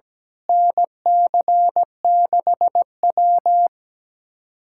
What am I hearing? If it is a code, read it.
Morse code, heard as a keyed tone, about 17 wpm, 700 Hz. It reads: NC6W
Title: USA Callsign DE Callsign - 4 Characters - 17wpm